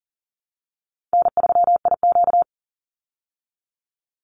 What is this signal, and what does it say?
Morse code, heard as a keyed tone, about 40 wpm, 700 Hz. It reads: N3IQ
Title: Call Signs Repeated 2x - 40wpm